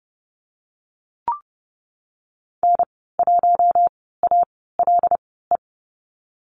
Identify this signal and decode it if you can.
Morse code, heard as a keyed tone, about 30 wpm, spaced as for 17 wpm, 700 Hz. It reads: N1ALE
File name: Lesson 19g - Letter w - Call Signs 30wpm